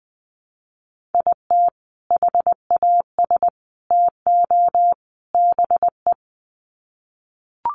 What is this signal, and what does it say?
Morse code, heard as a keyed tone, about 20 wpm, 700 Hz. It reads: IT HAS TO BE
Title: Sets of 4 Words 20wpm